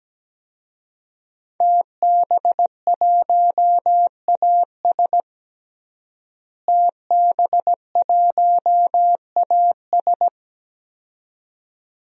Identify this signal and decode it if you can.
Morse code, heard as a keyed tone, about 17 wpm, 700 Hz. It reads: TB1AS TB1AS
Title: DXCC-100 Call Signs - 5 Characters - Repeated 2x - 17wpm